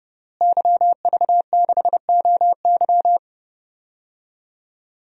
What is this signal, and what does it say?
Morse code, heard as a keyed tone, about 30 wpm, 700 Hz. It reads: YV6OY
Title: DXCC-100 Call Signs - 5 Characters - Repeated 2x - 30wpm